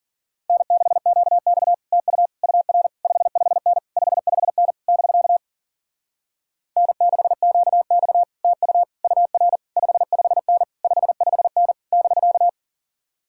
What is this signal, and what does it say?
Morse code, heard as a keyed tone, about 52 wpm, 700 Hz. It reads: N6QX TU UR 55N 55N <BK> N6QX TU UR 55N 55N <BK>